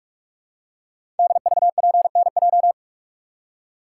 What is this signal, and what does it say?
Morse code, heard as a keyed tone, about 44 wpm, 700 Hz. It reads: DVPNJ
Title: Code Groups - 5 Letters - 45wpm